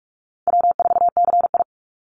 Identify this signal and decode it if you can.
Morse code, heard as a keyed tone, about 44 wpm, 700 Hz. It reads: W4CI
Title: Call Signs Repeated 2x - 45wpm